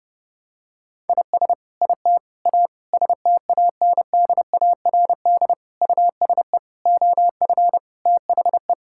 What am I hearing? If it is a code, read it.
Morse code, heard as a keyed tone, about 30 wpm, 700 Hz. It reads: IS IT A STANDARD USE OF THE
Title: Sentences from Top 1000 Words - Review Part 6 - 30wpm